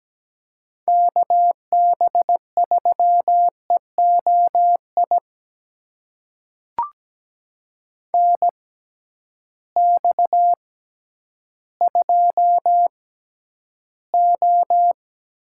Morse code, heard as a keyed tone, about 17 wpm, spaced as for 10 wpm, 700 Hz. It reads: KB3EOI NX2O
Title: USA Call Signs - 6x Character Spacing - 17wpm